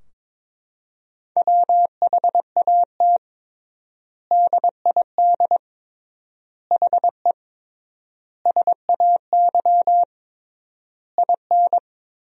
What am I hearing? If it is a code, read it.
Morse code, heard as a keyed tone, about 22 wpm, 700 Hz. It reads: WHAT DID HE SAY IN
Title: 3x Word-Spacing - Sentences from Top 300 Words - 22wpm